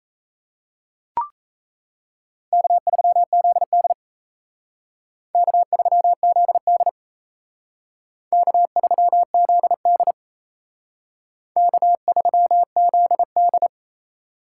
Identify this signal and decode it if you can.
Morse code, heard as a keyed tone, about 35 wpm, 700 Hz. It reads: K3ZD K3ZD K3ZD K3ZD
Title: Speed Racing - USA 1x2 Call Signs 28wpm